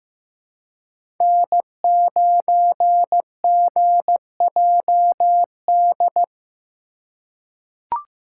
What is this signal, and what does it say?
Morse code, heard as a keyed tone, about 15 wpm, 700 Hz. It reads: N9GJD
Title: USA Call Signs - 5 Characters - 3x Character Spacing - 15wpm